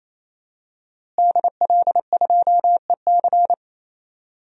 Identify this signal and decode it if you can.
Morse code, heard as a keyed tone, about 27 wpm, 700 Hz. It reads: DL2EC
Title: DXCC-100 Call Signs - 5 Characters - Repeated 3x - 28wpm